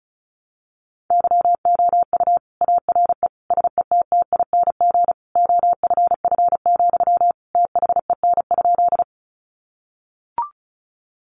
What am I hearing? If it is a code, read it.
Morse code, heard as a keyed tone, about 34 wpm, 700 Hz. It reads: YOU ARE SETTING OFF, THEN?
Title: Sentences from Top 1300 Words 35wpm